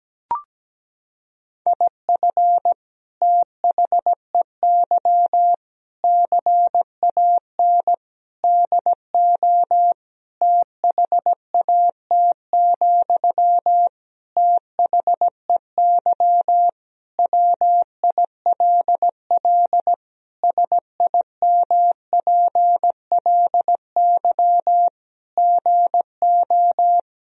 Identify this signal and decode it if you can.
Morse code, heard as a keyed tone, about 17 wpm, 700 Hz. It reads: IF THEY CAN DO THAT, THEY WILL SIMPLY GO
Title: Sentences from Top 1000 Words - Review Part 4 - 17wpm